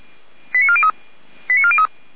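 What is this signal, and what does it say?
Five-tone selective call (ZVEI), heard as quick runs of five tones, tones of about 70 ms. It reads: 89403 89403
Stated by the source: ZVEI five-tone selective call received via SDR, Feuerwehr Orlamünde fire brigade.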